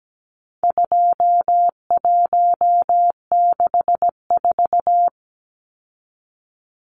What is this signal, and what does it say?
Morse code, heard as a keyed tone, about 17 wpm, 700 Hz. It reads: 2164
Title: Numbers - 4-Digit 17wpm